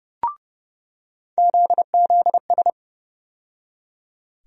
Morse code, heard as a keyed tone, about 30 wpm, 700 Hz. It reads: ZZS
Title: USA Call Sign Postfixes - 3 Letters 30wpm